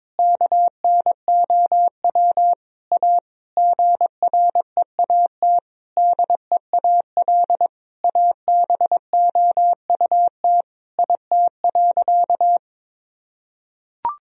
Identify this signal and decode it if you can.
Morse code, heard as a keyed tone, about 22 wpm, 700 Hz. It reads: KNOW A GREAT DEAL ABOUT IT.
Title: Sentences from Top 1000 Words - Review Part 1 - 22wpm